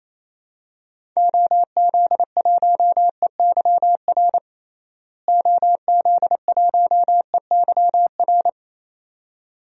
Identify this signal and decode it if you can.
Morse code, heard as a keyed tone, about 28 wpm, 700 Hz. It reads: OZ1EYR OZ1EYR
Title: DXCC-100 Call Signs - 6 Characters - Repeated 2x - 28wpm